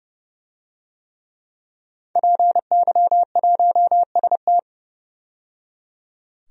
Morse code, heard as a keyed tone, about 30 wpm, 700 Hz. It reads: PY1ST
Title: DXCC-100 Call Signs - 5 Characters - 30wpm